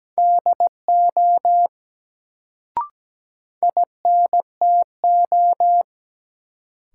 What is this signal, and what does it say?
Morse code, heard as a keyed tone, about 17 wpm, 700 Hz. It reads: DO INTO
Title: Sets of 2 Words - New Word Drill - 17wpm